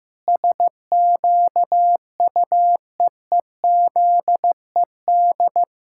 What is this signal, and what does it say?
Morse code, heard as a keyed tone, about 15 wpm, 700 Hz. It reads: SQUEEZED
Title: Pangram Challenge v2 - 15wpm